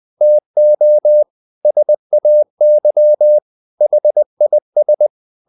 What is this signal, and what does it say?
Morse code, heard as a keyed tone, about 20 wpm, 600 Hz. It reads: TO SAY HIS
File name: the old man and the sea / the old man and the sea part 1 20wpm